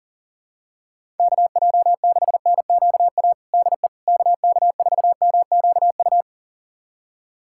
Morse code, heard as a keyed tone, about 40 wpm, 700 Hz. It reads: KJ6NQA DE KK4MQU
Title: USA Callsign DE Callsign 40wpm